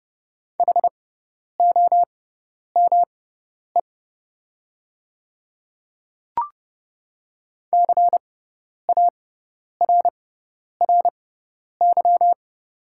Morse code, heard as a keyed tone, about 30 wpm, spaced as for 10 wpm, 700 Hz. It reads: HOME CARRY
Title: 6x Character Spacing - Top 500 Words - 30wpm